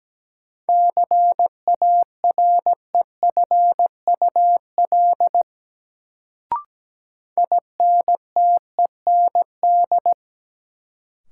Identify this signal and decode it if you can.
Morse code, heard as a keyed tone, about 17 wpm, 700 Hz. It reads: CAREFUL INTEND
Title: Sentences from Top 1600 Words - New Word Drill - 17wpm